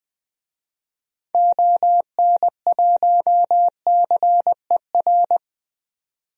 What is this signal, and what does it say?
Morse code, heard as a keyed tone, about 20 wpm, 700 Hz. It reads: ON1CER
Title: DXCC-100 Call Signs - 6 Characters - 20wpm